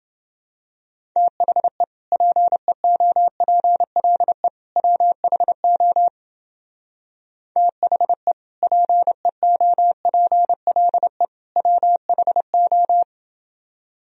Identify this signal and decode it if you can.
Morse code, heard as a keyed tone, about 28 wpm, 700 Hz. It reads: THE PEOPLE WHO THE PEOPLE WHO